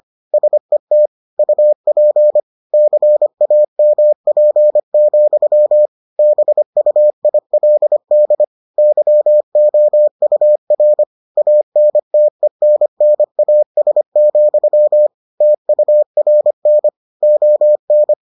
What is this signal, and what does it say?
Morse code, heard as a keyed tone, about 25 wpm, 600 Hz. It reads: SET UP CAMP, BUILD YOUR ANTENNAS, TURN ON